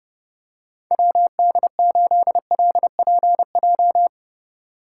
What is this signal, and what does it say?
Morse code, heard as a keyed tone, about 31 wpm, 700 Hz. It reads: WD8LPJ
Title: USA Call Signs - 6 Characters - 3x Character Spacing - 30wpm